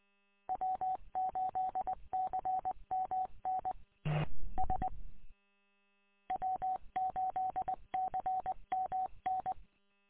Morse code, heard as a keyed tone, about 20 wpm, 750 Hz. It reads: W8CMN S W8CMN
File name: w8cmn- -combined